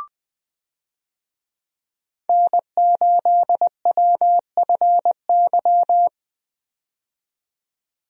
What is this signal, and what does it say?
Morse code, heard as a keyed tone, about 20 wpm, 700 Hz. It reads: N8WFY
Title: USA Call Signs - 5 Characters - 20wpm